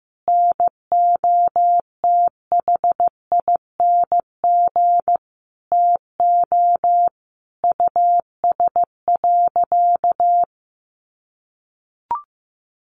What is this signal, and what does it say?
Morse code, heard as a keyed tone, about 15 wpm, 700 Hz. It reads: NOTHING TO US.
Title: Sentences from Top 500 Words 15wpm